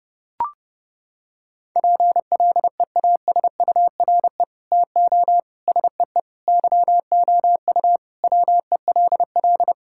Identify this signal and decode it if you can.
Morse code, heard as a keyed tone, about 30 wpm, 700 Hz. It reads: PLEASURE TO SEE YOU WELL
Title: Sentences from Top 1600 Words 30wpm